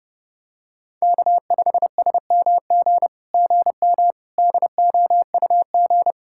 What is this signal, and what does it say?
Morse code, heard as a keyed tone, about 30 wpm, 700 Hz. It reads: K5SMG GM DOUG
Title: SOTA QSO Practice 30wpm